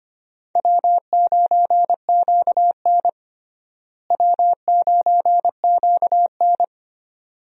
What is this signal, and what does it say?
Morse code, heard as a keyed tone, about 25 wpm, 700 Hz. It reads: W9QN W9QN
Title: USA Call Signs - 4 Characters - Repeated 2x - 25wpm